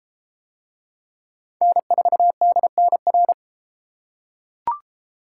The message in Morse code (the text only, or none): N4DNR